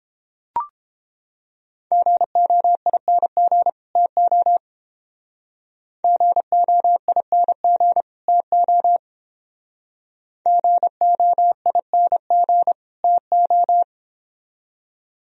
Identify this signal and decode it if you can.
Morse code, heard as a keyed tone, about 29 wpm, 700 Hz. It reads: GOING TO GOING TO GOING TO